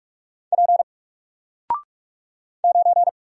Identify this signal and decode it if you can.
Morse code, heard as a keyed tone, about 46 wpm, 700 Hz. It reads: P 9
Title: Single Letter-Number Alt 45wpm